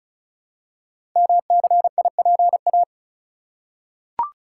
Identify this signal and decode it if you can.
Morse code, heard as a keyed tone, about 36 wpm, 700 Hz. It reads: MCIPA
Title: Code Groups - 5 Letters - 35wpm